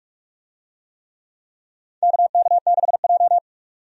Morse code, heard as a keyed tone, about 46 wpm, 700 Hz. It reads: KK6J
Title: USA Call Signs - 4 Characters - 45wpm